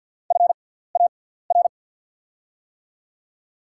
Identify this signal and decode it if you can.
Morse code, heard as a keyed tone, about 48 wpm, spaced as for 17 wpm, 700 Hz. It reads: FAR
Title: 6x Character Spacing - Top 300 Words - 50wpm